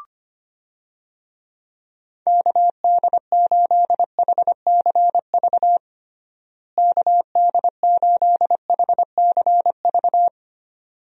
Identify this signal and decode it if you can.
Morse code, heard as a keyed tone, about 25 wpm, 700 Hz. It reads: KD8HCV KD8HCV
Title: USA Call Signs - 6 Characters - Repeated 2x - 25wpm